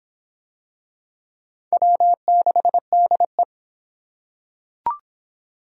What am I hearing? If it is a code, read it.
Morse code, heard as a keyed tone, about 24 wpm, 700 Hz. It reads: W6DE